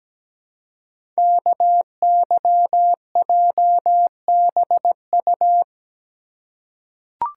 Morse code, heard as a keyed tone, about 17 wpm, 700 Hz. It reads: KYJBU
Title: Code Groups - 5 Letters - 17wpm